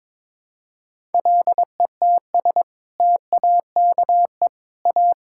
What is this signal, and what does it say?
Morse code, heard as a keyed tone, about 22 wpm, 700 Hz. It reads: LETS TAKE A
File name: practice / 22wpm / Sets of 6 Words - 22wpm